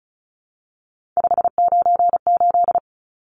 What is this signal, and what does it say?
Morse code, heard as a keyed tone, about 34 wpm, 700 Hz. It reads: 598
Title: Numbers - 3-Digit 35wpm